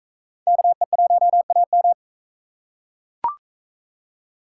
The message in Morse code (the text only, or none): KE1AM